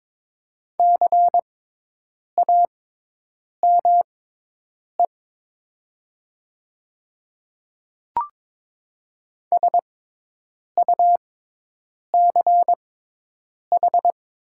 Morse code, heard as a keyed tone, about 22 wpm, spaced as for 8 wpm, 700 Hz. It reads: CAME SUCH
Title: 6x Character Spacing - Top 200 Words - 22wpm